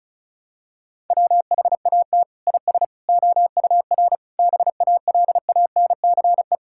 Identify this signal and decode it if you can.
Morse code, heard as a keyed tone, about 36 wpm, 700 Hz. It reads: WHAT IS OUR BALANCE
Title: Sentences from Top 1000 Words - Review Part 6 - 35wpm